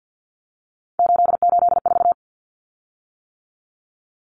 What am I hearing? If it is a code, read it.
Morse code, heard as a keyed tone, about 50 wpm, 700 Hz. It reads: 884
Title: Common Cut Numbers - 3-Digit 50wpm